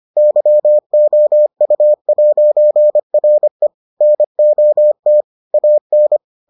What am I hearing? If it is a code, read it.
Morse code, heard as a keyed tone, about 25 wpm, 600 Hz. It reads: YOU'RE NOT AN